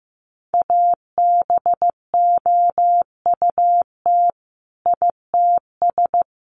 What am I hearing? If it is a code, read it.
Morse code, heard as a keyed tone, about 15 wpm, 700 Hz. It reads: ABOUT ITS
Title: Sentences from Top 1200 Words 15wpm